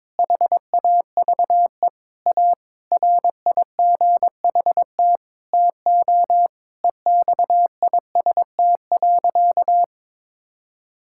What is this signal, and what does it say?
Morse code, heard as a keyed tone, about 22 wpm, 700 Hz. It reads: HAVE A RIGHT TO EXIST.
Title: Sentences from Top 1000 Words - Review Part 5 - 22wpm